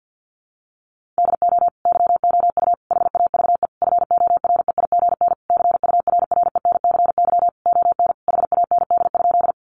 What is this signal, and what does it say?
Morse code, heard as a keyed tone, about 50 wpm, 700 Hz. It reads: DO YOU HAVE FOREIGN CURRENCY ON HAND?
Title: Sentences from Top 1500 Words 50wpm